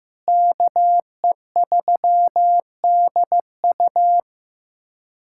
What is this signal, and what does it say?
Morse code, heard as a keyed tone, about 15 wpm, 700 Hz. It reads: KE3DU
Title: USA Callsign DE Callsign - 5 Characters - 15wpm